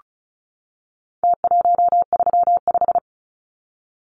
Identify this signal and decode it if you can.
Morse code, heard as a keyed tone, about 34 wpm, 700 Hz. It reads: T135
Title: Numbers - Contest Style - 4-Digit 35wpm